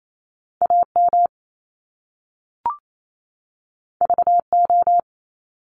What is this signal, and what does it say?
Morse code, heard as a keyed tone, about 28 wpm, 700 Hz. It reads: A M V O
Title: Top Two Letter Combinations in Words 28wpm